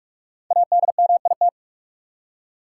Morse code, heard as a keyed tone, about 41 wpm, 700 Hz. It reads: ADMIT
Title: Five Letter Words - 45wpm